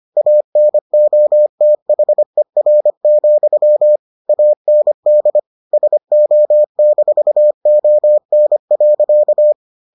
Morse code, heard as a keyed tone, about 25 wpm, 600 Hz. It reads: ANOTHER, AND SO-ON.